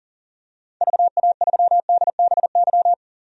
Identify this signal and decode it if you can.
Morse code, heard as a keyed tone, about 40 wpm, 700 Hz. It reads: VA3DBY
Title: DXCC-100 Call Signs - 6 Characters - 40wpm